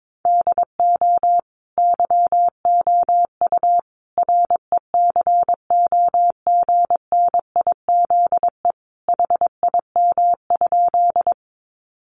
Morse code, heard as a keyed tone, about 22 wpm, 700 Hz. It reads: DO YOU RECOGNIZE HIM?
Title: Sentences from Top 1400 Words 22wpm